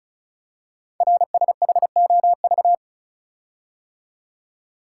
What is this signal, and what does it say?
Morse code, heard as a keyed tone, about 36 wpm, 700 Hz. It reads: RSHOV
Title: Code Groups - 5 Letter:Numbers - 35wpm